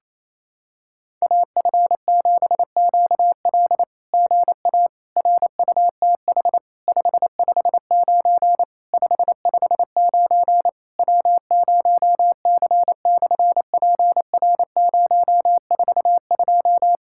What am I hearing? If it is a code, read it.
Morse code, heard as a keyed tone, about 27 wpm, 700 Hz. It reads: AF7QL GA RUTH 559 559 W0C/PR042
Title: SOTA QSO Practice 28wpm